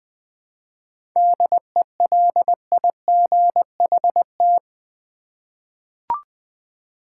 Morse code, heard as a keyed tone, about 20 wpm, 690 Hz. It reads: DELIGHT
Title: 7 Letter Words - First 3 Letters Given - 20wpm